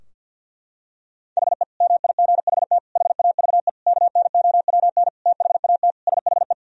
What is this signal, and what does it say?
Morse code, heard as a keyed tone, about 50 wpm, 700 Hz. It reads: HE MIGHT HAVE KNOWN THAT SHE
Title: Sentences from Top 1000 Words - Review Part 3 - 50wpm